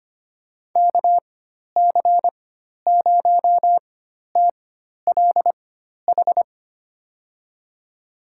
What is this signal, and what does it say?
Morse code, heard as a keyed tone, about 25 wpm, spaced as for 12 wpm, 700 Hz. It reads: KC0TLH
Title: USA Call Signs - 6 Characters - 4x Character Spacing - 25wpm